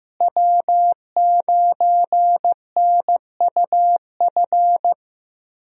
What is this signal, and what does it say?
Morse code, heard as a keyed tone, about 15 wpm, 700 Hz. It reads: W9NUF
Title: USA Call Signs - 5 Characters - Repeated 3x - 15wpm